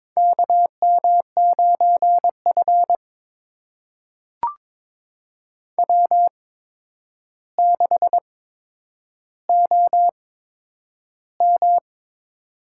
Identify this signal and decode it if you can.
Morse code, heard as a keyed tone, about 22 wpm, spaced as for 9 wpm, 700 Hz. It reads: KM9F W6OM
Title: USA Call Signs - 4 Characters - 8x Character Spacing - 22wpm